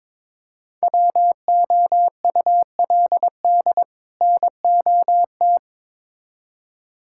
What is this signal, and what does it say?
Morse code, heard as a keyed tone, about 22 wpm, 700 Hz. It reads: WOULD NOT
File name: Build A Phrase - 2-Word Phrases - 22wpm